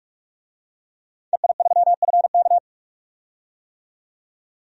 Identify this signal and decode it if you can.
Morse code, heard as a keyed tone, about 46 wpm, 700 Hz. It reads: EI3FK
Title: DXCC-100 Call Signs - 5 Characters - 45wpm